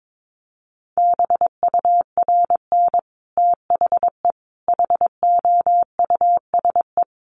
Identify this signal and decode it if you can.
Morse code, heard as a keyed tone, about 22 wpm, 700 Hz. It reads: BURN THE HOUSE
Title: Sentences from Top 1300 Words 22wpm